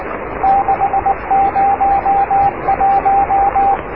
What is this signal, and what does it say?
Morse code, heard as a keyed tone, about 19 wpm, 750 Hz, with radio noise. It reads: 601
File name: oscar 7 de k5dze section 2